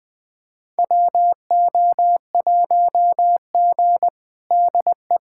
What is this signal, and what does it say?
Morse code, heard as a keyed tone, about 20 wpm, 700 Hz. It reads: WO1G DE
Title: USA Callsign DE Callsign - 4 Characters - 20wpm